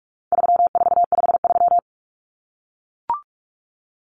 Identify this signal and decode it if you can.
Morse code, heard as a keyed tone, about 44 wpm, 700 Hz. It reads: 3453
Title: Numbers - 4-Digit 45wpm